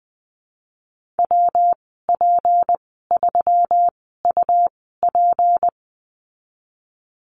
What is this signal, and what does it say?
Morse code, heard as a keyed tone, about 20 wpm, spaced as for 15 wpm, 700 Hz. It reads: WP3UP
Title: 193 Lesson 31g - Letter v - Call Signs 20wpm